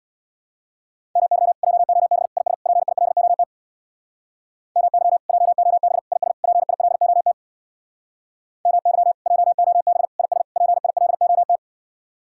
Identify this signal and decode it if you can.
Morse code, heard as a keyed tone, about 62 wpm, 700 Hz. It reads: MY JOB IS PILOT MY JOB IS PILOT MY JOB IS PILOT
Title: Speed-Racing Common QSO Phrases Encore - 50wpm